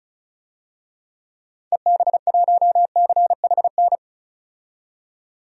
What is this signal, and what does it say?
Morse code, heard as a keyed tone, about 36 wpm, 700 Hz. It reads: EB1CHN
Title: DXCC-100 Call Signs - 6 Characters - 35wpm